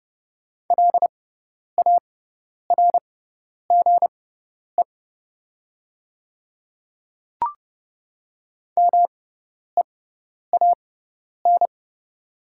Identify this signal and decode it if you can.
Morse code, heard as a keyed tone, about 30 wpm, spaced as for 10 wpm, 700 Hz. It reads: LARGE MEAN
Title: 6x Character Spacing - Top 200 Words - 30wpm